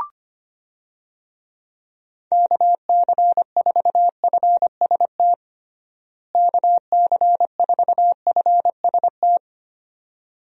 Morse code, heard as a keyed tone, about 25 wpm, 700 Hz. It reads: KC4FST KC4FST
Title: USA Call Signs - 6 Characters - Repeated 2x - 25wpm